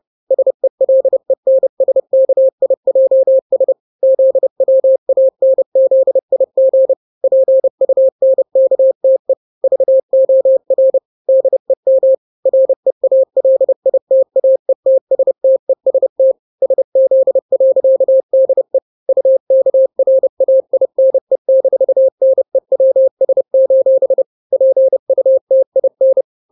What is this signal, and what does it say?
Morse code, heard as a keyed tone, about 30 wpm, 530 Hz. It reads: SELENSKIJS ZWANZIG PUNKTE VOR DEM REALITAETSTEST SZ.DE UKRAINE-NEWS: PUTIN
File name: daily / 29wpm